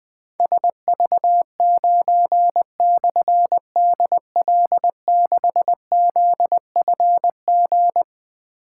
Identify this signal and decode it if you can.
Morse code, heard as a keyed tone, about 20 wpm, 700 Hz. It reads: SV9/DL6ZFG
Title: Difficult Call Signs 20wpm